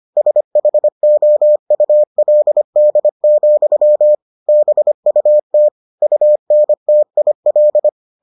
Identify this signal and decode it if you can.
Morse code, heard as a keyed tone, about 25 wpm, 600 Hz. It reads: SHOULD, BUT UNTIL